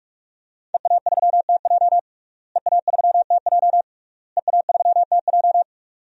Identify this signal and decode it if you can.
Morse code, heard as a keyed tone, about 44 wpm, 700 Hz. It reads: EA3TJ EA3TJ EA3TJ
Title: DXCC-100 Call Signs - 5 Characters - Repeated 3x - 45wpm